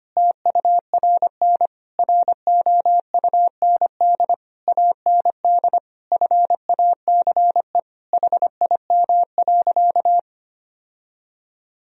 Morse code, heard as a keyed tone, about 25 wpm, 700 Hz. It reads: TURN ROUND AND FACE HIM.
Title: Sentences from Top 1000 Words - Review Part 2 - 25wpm